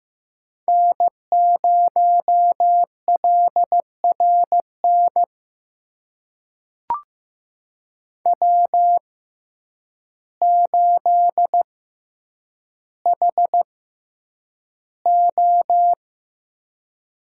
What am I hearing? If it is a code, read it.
Morse code, heard as a keyed tone, about 15 wpm, spaced as for 8 wpm, 700 Hz. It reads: N0LRN W8HO
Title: USA Call Signs - 5 Characters - 6x Character Spacing - 15wpm